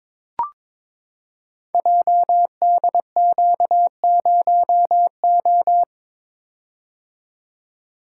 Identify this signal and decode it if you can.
Morse code, heard as a keyed tone, about 23 wpm, 700 Hz. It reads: JDQ0O
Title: Code Groups - 5 Letter:Numbers - 22wpm